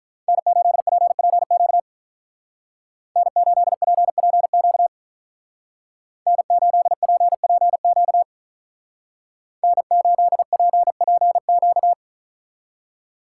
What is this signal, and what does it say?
Morse code, heard as a keyed tone, about 43 wpm, 700 Hz. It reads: N8PPQ N8PPQ N8PPQ N8PPQ
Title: Speed Racing - USA 1x3 Call Signs 35wpm